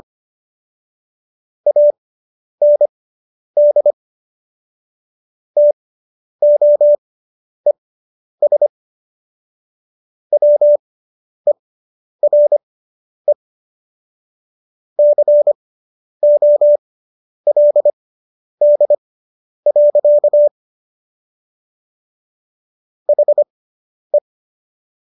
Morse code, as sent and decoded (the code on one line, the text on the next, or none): .- -. -.. / - --- . ... / .-- . .-. / -.-. --- .-.. -.. .-.-.- / .... .
AND TOES WERE COLD. HE